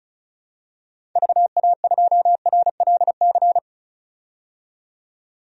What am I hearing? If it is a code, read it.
Morse code, heard as a keyed tone, about 34 wpm, 700 Hz. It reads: VA2RLC
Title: DXCC-100 Call Signs - 6 Characters - 35wpm